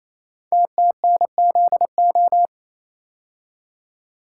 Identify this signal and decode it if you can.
Morse code, heard as a keyed tone, about 27 wpm, 700 Hz. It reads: TTNZO
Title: Code Groups - 5 Letters - 28wpm